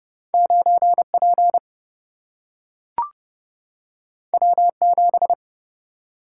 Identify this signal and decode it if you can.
Morse code, heard as a keyed tone, about 30 wpm, 700 Hz. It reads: 9P W7
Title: Single-Letter Single-Number 30wpm